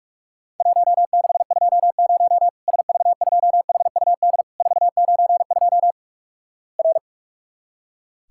Morse code, heard as a keyed tone, about 46 wpm, 700 Hz. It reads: 1620 SV2HUD 492 R
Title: Worked All Europe DX Contest - QTC Exchange - 45wpm